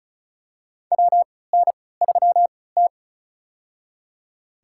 Morse code, heard as a keyed tone, about 34 wpm, spaced as for 20 wpm, 700 Hz. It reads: WN3T
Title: USA Call Signs - 3x Character Spacing - 35wpm